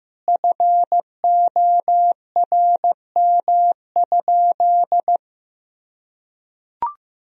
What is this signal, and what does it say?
Morse code, heard as a keyed tone, about 15 wpm, 700 Hz. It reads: FORM?
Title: Sentences from Top 1000 Words - Review Part 1 - 15wpm